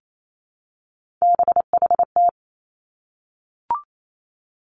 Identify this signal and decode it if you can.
Morse code, heard as a keyed tone, about 28 wpm, 700 Hz. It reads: BHT
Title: Three Challenging Letter-Number Combinations 28wpm